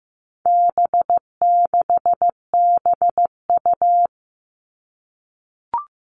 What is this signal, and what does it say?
Morse code, heard as a keyed tone, about 15 wpm, 700 Hz. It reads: B6BU
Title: All Cut Numbers - 4-Digit 15wpm